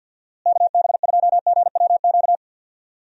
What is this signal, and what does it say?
Morse code, heard as a keyed tone, about 51 wpm, 700 Hz. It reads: KB2GWQ
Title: USA Call Signs - 6 Characters - Repeated 3x - 50wpm